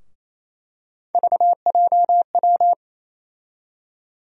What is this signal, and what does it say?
Morse code, heard as a keyed tone, about 28 wpm, 700 Hz. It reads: VJW
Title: USA Call Sign Postfixes - 3 Letters 28wpm